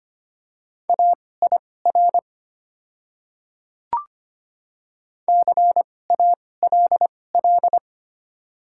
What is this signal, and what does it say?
Morse code, heard as a keyed tone, about 25 wpm, spaced as for 18 wpm, 700 Hz. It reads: AIR CALL
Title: 2x Character Spacing - Top 500 Words - 25wpm